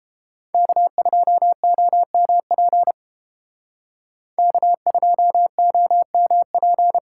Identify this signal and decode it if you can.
Morse code, heard as a keyed tone, about 30 wpm, 700 Hz. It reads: K2OMP K2OMP